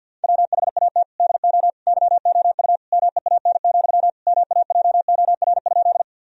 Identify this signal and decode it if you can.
Morse code, heard as a keyed tone, about 50 wpm, 700 Hz. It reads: WHAT DO YOU MEAN, MAJOR?
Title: Sentences from Top 1000 Words - Review Part 4 - 50wpm